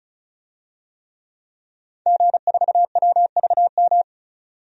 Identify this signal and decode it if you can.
Morse code, heard as a keyed tone, about 36 wpm, 700 Hz. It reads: G4WVM
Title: DXCC-100 Call Signs - 5 Characters - 35wpm